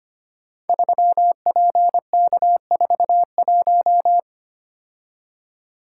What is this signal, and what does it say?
Morse code, heard as a keyed tone, about 25 wpm, 700 Hz. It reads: 3PK41
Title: Code Groups - 5 Letter:Numbers - 25wpm